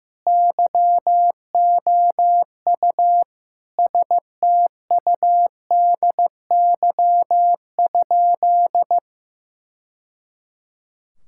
Morse code, heard as a keyed tone, about 15 wpm, 700 Hz. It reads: YOU STUDY?
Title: Sentences from Top 1000 Words - Review Part 6 - 15wpm